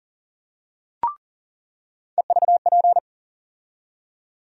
Morse code, heard as a keyed tone, about 40 wpm, 700 Hz. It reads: EVP
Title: USA Call Sign Postfixes - 3 Letters 40wpm